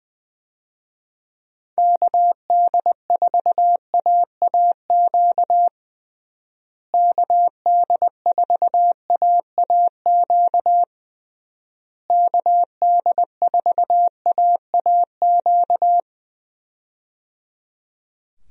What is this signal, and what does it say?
Morse code, heard as a keyed tone, about 20 wpm, 700 Hz. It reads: KD4AAQ KD4AAQ KD4AAQ
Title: USA Call Signs - 6 Characters - Repeated 3x - 20wpm